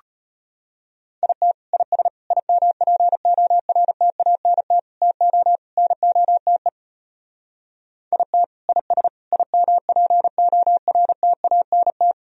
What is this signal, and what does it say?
Morse code, heard as a keyed tone, about 33 wpm, 700 Hz. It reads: IT IS IMPORTANT TO NOTE IT IS IMPORTANT